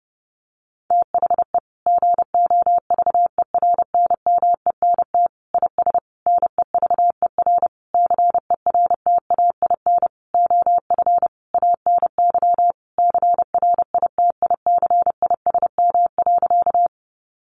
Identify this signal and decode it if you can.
Morse code, heard as a keyed tone, about 30 wpm, 700 Hz. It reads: THE GOVERNMENT IS NEVER CERTAIN OF ANY CRITICISM.
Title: Sentences from Top 1300 Words 30wpm